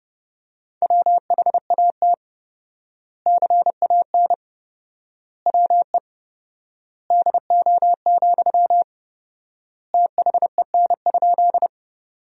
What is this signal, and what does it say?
Morse code, heard as a keyed tone, about 32 wpm, 700 Hz. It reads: WHAT CAN WE DO, THEN?
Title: 4x Word-Spacing - Sentences from Top 100 Words - 30wpm